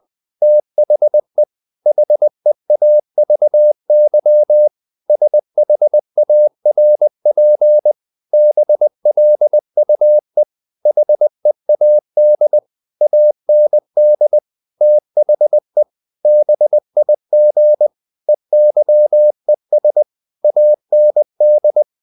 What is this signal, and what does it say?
Morse code, heard as a keyed tone, about 20 wpm, 600 Hz. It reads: THE HEAVY SHARP BLUE HEAD AND THE BIG EYES AND
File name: the old man and the sea part 2 20wpm